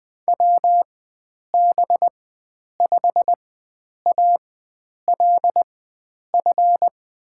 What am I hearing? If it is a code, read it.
Morse code, heard as a keyed tone, about 21 wpm, spaced as for 9 wpm, 700 Hz. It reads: WB5ALF
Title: USA Call Signs - 4x Character Spacing - 20wpm